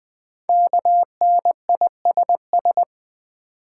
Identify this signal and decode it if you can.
Morse code, heard as a keyed tone, about 20 wpm, 700 Hz. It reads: KNISS
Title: Code Groups - 5 Letters - 20wpm